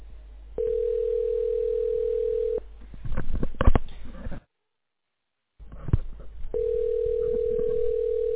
Ringback tone (North America), 440 + 480 Hz.